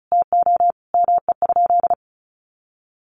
Morse code, heard as a keyed tone, about 34 wpm, 700 Hz. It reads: TO ME?
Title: Sentences from Top 500 Words 35wpm